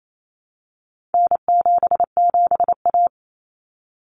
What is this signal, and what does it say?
Morse code, heard as a keyed tone, about 28 wpm, 700 Hz. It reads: N77A
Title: Common Cut Numbers - 4-Digit 28wpm